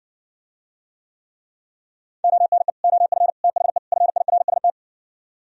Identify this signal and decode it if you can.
Morse code, heard as a keyed tone, about 68 wpm, 700 Hz. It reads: ONE OF THE FIRST